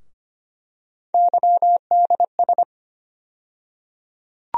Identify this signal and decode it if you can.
Morse code, heard as a keyed tone, about 25 wpm, 700 Hz. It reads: YDS
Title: USA Call Sign Postfixes - 3 Letters 25wpm